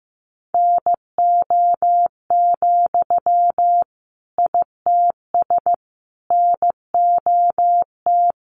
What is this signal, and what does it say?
Morse code, heard as a keyed tone, about 15 wpm, 700 Hz. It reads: NO, ITS NOT
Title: Sentences from Top 600 Words 15wpm